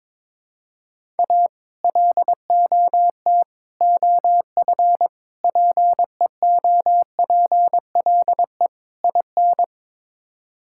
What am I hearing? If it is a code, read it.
Morse code, heard as a keyed tone, about 22 wpm, 700 Hz. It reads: A LOT OF PEOPLE IN
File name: practice / 22wpm / Build A Phrase - 5-Word Phrases - 22wpm